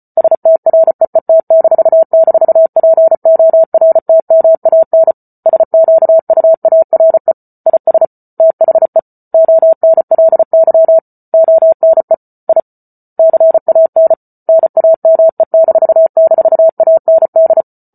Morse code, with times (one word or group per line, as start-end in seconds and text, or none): 0.2-5.1 STREET--PORTMAN
5.5-7.3 SQUARE
7.7-8.1 IS
8.4-9.0 THE
9.3-11.0 ONLY
11.3-12.1 ONE
12.5-12.6 I
13.2-14.1 CAN
14.5-17.6 NAME--AND